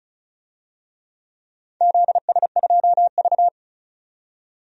Morse code, heard as a keyed tone, about 36 wpm, 700 Hz. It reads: ZS2V
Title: DXCC-100 Call Signs - 4 Characters - 35wpm